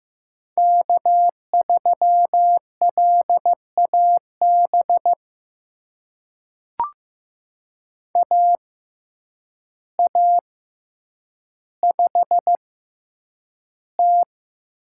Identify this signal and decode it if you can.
Morse code, heard as a keyed tone, about 15 wpm, spaced as for 8 wpm, 700 Hz. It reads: K3LAB AA5T
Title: USA Call Signs - 6x Character Spacing - 15wpm